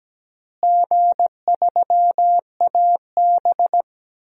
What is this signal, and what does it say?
Morse code, heard as a keyed tone, about 17 wpm, 700 Hz. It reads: G3AB
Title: DXCC-100 Call Signs - 4 Characters - Repeated 3x - 17wpm